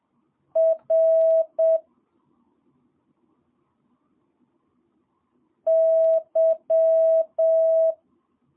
Morse code, heard as a keyed tone, about 7 wpm, 650 Hz. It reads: R Y
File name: jh8cbh / 2022cw